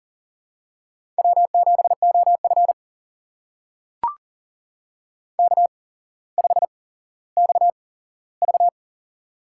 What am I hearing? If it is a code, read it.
Morse code, heard as a keyed tone, about 40 wpm, spaced as for 17 wpm, 700 Hz. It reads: W7OF K5XV
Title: USA Call Signs - 4 Characters - 8x Character Spacing - 40wpm